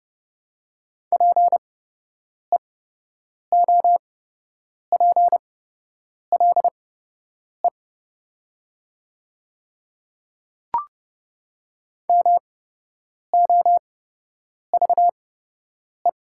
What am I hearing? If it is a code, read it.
Morse code, heard as a keyed tone, about 30 wpm, spaced as for 8 wpm, 700 Hz. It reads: PEOPLE MOVE
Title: 8x Character Spacing - Top 300 Words - 30wpm